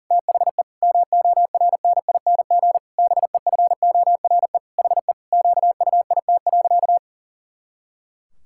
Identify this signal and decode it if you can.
Morse code, heard as a keyed tone, about 40 wpm, 700 Hz. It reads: THE MORNING BEFORE HE QUIT.
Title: Pangram Challenge v2 - 40wpm